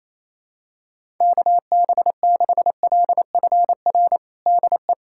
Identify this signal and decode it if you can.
Morse code, heard as a keyed tone, about 28 wpm, 700 Hz. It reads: KB6LFR DE
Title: USA Callsign DE Callsign 28wpm